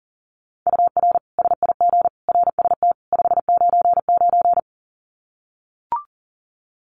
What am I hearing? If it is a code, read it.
Morse code, heard as a keyed tone, about 40 wpm, 700 Hz. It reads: UR SIG RST 599
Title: Common QSO Phrases 40wpm